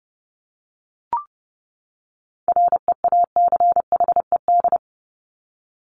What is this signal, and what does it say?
Morse code, heard as a keyed tone, about 30 wpm, 700 Hz. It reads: REACHED
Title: ED Suffix - Top 100 Words with Base Words 30wpm